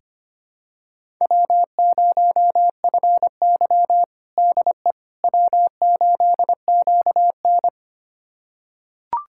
Morse code, heard as a keyed tone, about 25 wpm, 700 Hz. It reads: W0FY DE W8QN
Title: USA Callsign DE Callsign - 4 Characters - 25wpm